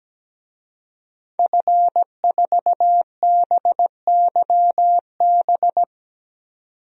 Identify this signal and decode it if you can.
Morse code, heard as a keyed tone, about 17 wpm, 700 Hz. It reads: F4BYB
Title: DXCC-100 Call Signs - 5 Characters - 17wpm